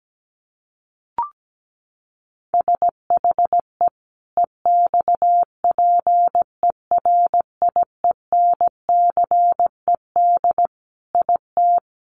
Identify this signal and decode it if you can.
Morse code, heard as a keyed tone, about 17 wpm, 700 Hz. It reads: SHE EXPERIENCED IT
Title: Sentences from Top 1100 Words 17wpm